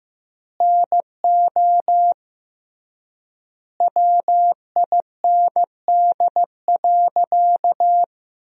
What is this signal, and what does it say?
Morse code, heard as a keyed tone, about 15 wpm, 700 Hz. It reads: NO WIND.
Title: 3x Word-Spacing - Sentences from Top 400 Words - 15wpm